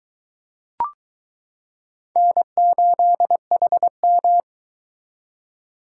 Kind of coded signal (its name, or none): Morse code